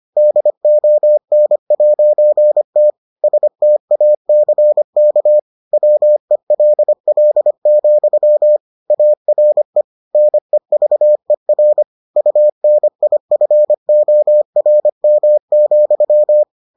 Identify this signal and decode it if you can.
Morse code, heard as a keyed tone, about 25 wpm, 600 Hz. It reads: DON'T STACK WELL, ARE NEVER UNIFORM,